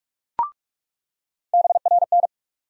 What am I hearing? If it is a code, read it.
Morse code, heard as a keyed tone, about 43 wpm, 700 Hz. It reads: BRN
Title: USA Call Sign Postfixes - 3 Letters 45wpm